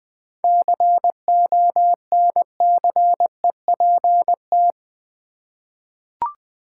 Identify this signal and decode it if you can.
Morse code, heard as a keyed tone, about 20 wpm, 700 Hz. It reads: CONCEPT
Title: Seven Letter Words - 20wpm